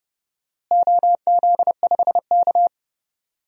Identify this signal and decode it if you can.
Morse code, heard as a keyed tone, about 30 wpm, 700 Hz. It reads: OZ5K
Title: DXCC-100 Call Signs - 4 Characters - Repeated 3x - 30wpm